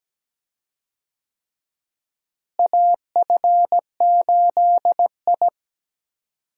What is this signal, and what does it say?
Morse code, heard as a keyed tone, about 17 wpm, 700 Hz. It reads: AF8I
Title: USA Call Signs - 4 Characters - Repeated 3x - 17wpm